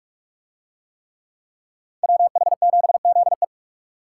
Morse code, heard as a keyed tone, about 44 wpm, 700 Hz. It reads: WH7ZE